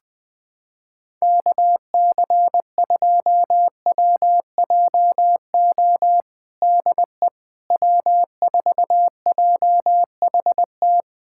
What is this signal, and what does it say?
Morse code, heard as a keyed tone, about 20 wpm, 700 Hz. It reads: KC2WJO DE W4JHT
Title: USA Callsign DE Callsign 20wpm